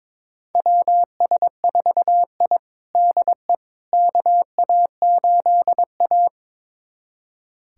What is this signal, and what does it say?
Morse code, heard as a keyed tone, about 22 wpm, 700 Hz. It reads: WS4I DE KA8A
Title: USA Callsign DE Callsign - 4 Characters - 22wpm